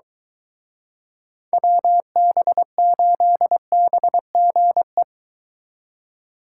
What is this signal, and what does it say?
Morse code, heard as a keyed tone, about 23 wpm, 700 Hz. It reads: WB8BGE